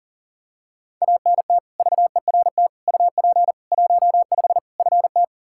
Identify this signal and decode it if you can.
Morse code, heard as a keyed tone, about 40 wpm, 700 Hz. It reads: ANT VERT UP 15 FT
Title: K1USN SST QSO Practice 40wpm